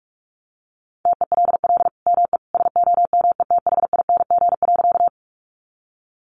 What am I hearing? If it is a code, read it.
Morse code, heard as a keyed tone, about 44 wpm, 700 Hz. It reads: TELL ME SOMETHING.
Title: Sentences from Top 600 Words 45wpm